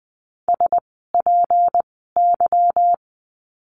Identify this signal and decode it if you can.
Morse code, heard as a keyed tone, about 19 wpm, spaced as for 14 wpm, 700 Hz. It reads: SPY
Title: 153 Lesson 25f - Letter p - Words 20wpm